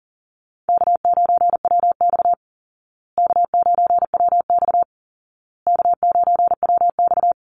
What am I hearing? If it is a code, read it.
Morse code, heard as a keyed tone, about 40 wpm, 700 Hz. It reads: K9WX K9WX K9WX
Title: Easy Call Signs Repeated 3x - 40wpm